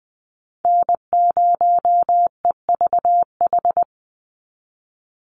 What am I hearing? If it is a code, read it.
Morse code, heard as a keyed tone, about 20 wpm, 700 Hz. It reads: N0EVH
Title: NA SOTA Chaser Callsigns 20wpm